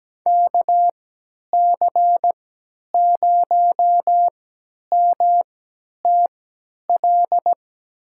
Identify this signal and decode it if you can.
Morse code, heard as a keyed tone, about 17 wpm, spaced as for 10 wpm, 700 Hz. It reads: KC0MTL
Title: USA Call Signs - 6 Characters - 3x Character Spacing - 17wpm